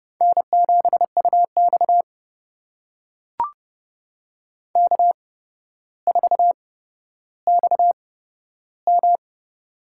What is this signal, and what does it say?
Morse code, heard as a keyed tone, about 30 wpm, spaced as for 13 wpm, 700 Hz. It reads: N7UX K4XM
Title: USA Call Signs - 4 Characters - 8x Character Spacing - 30wpm